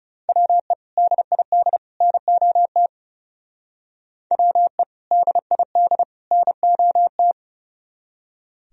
Morse code, heard as a keyed tone, about 39 wpm, 700 Hz. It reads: WE DID NOT WE DID NOT